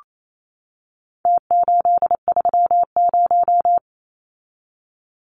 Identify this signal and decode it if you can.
Morse code, heard as a keyed tone, about 28 wpm, 700 Hz. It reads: T830
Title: Numbers - Contest Style - 4-Digit 28wpm